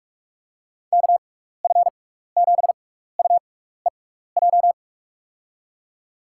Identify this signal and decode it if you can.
Morse code, heard as a keyed tone, about 46 wpm, spaced as for 16 wpm, 700 Hz. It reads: KF7UEJ
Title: USA Call Signs - 6x Character Spacing - 45wpm